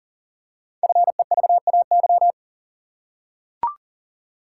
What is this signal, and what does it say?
Morse code, heard as a keyed tone, about 40 wpm, 700 Hz. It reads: FEVAY
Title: Code Groups - 5 Letter:Numbers - 40wpm